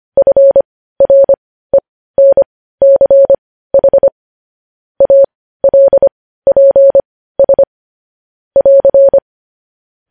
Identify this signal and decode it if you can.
Morse code, heard as a keyed tone, about 25 wpm, spaced as for 15 wpm, 550 Hz. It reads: FRENCH ALPS <AR>
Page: news-headlines-in-morse-code-at-15-wpm